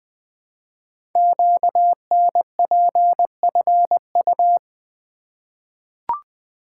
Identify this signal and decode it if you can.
Morse code, heard as a keyed tone, about 20 wpm, 700 Hz. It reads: QNPFU
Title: Code Groups - 5 Letters - 20wpm